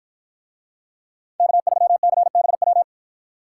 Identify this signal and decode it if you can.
Morse code, heard as a keyed tone, about 53 wpm, 700 Hz. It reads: K3CBW